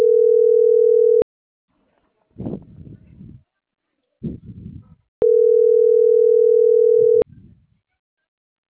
Ringback tone (North America), 440 + 480 Hz.